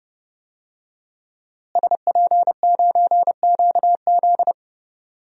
Morse code, heard as a keyed tone, about 30 wpm, 700 Hz. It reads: SP9QZ
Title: DXCC-100 Call Signs - 5 Characters - Repeated 3x - 30wpm